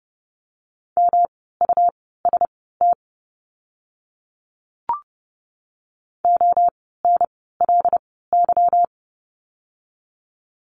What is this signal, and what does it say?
Morse code, heard as a keyed tone, about 30 wpm, spaced as for 17 wpm, 700 Hz. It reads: MUST ONLY
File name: course / 30:10wpm / 237 Lesson 38f - Letter q - Words 30wpm